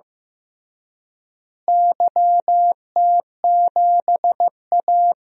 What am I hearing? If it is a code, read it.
Morse code, heard as a keyed tone, about 15 wpm, 700 Hz. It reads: YT7A
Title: DXCC-100 Call Signs - 4 Characters - Repeated 3x - 15wpm